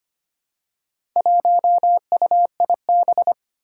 Morse code, heard as a keyed tone, about 25 wpm, 700 Hz. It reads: 1UIB
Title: Four Challenging Letter-Number Combinations 25wpm